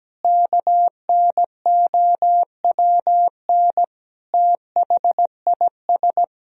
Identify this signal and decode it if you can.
Morse code, heard as a keyed tone, about 17 wpm, 700 Hz. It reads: KNOWN THIS
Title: Sentences from Top 1000 Words - Review Part 3 - 17wpm